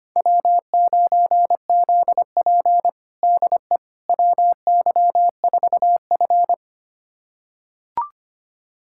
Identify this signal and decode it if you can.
Morse code, heard as a keyed tone, about 25 wpm, 700 Hz. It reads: W9ZP DE WY4F
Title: USA Callsign DE Callsign - 4 Characters - 25wpm